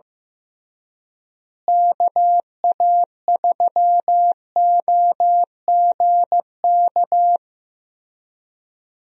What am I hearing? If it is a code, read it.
Morse code, heard as a keyed tone, about 15 wpm, 700 Hz. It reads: KA3OGK
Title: USA Call Signs - 6 Characters - Repeated 3x - 15wpm